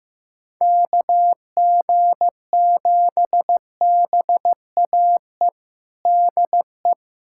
Morse code, heard as a keyed tone, about 15 wpm, 700 Hz. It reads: KG7BAE DE
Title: USA Callsign DE Callsign - 6 Characters - 15wpm